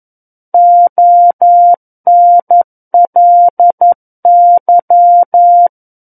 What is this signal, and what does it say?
Morse code, heard as a keyed tone, about 11 wpm, 700 Hz. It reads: ONLY
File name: 11 WPM